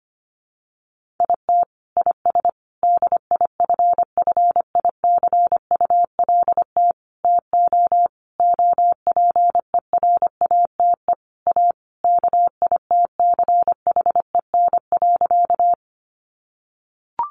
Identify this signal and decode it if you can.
Morse code, heard as a keyed tone, about 25 wpm, 700 Hz. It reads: IT IS DIFFICULT TO OPERATE A KITCHEN.
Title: Sentences from Top 1500 Words 25wpm